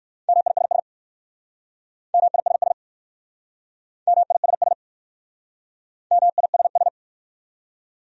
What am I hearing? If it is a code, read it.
Morse code, heard as a keyed tone, about 58 wpm, 700 Hz. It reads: MISS MISS MISS MISS